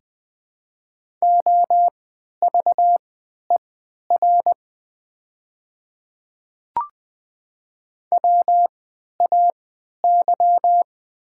Morse code, heard as a keyed tone, about 20 wpm, spaced as for 11 wpm, 700 Hz. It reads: OVER WAY
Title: 3x Character Spacing - Top 100 Words - 20wpm